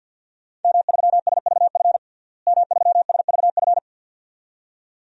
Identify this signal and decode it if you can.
Morse code, heard as a keyed tone, about 52 wpm, 700 Hz. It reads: M3SVF M3SVF
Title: DXCC-100 Call Signs - 5 Characters - Repeated 2x - 50wpm